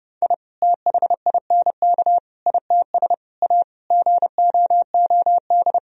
Morse code, heard as a keyed tone, about 30 wpm, 700 Hz. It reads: I THINK ITS A GOOD
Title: Sentences from Top 1000 Words - Review Part 1 - 30wpm